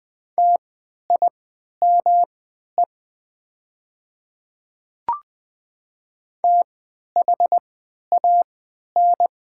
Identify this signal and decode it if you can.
Morse code, heard as a keyed tone, about 20 wpm, spaced as for 11 wpm, 700 Hz. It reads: TIME THAN
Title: 3x Character Spacing - Top 300 Words - 20wpm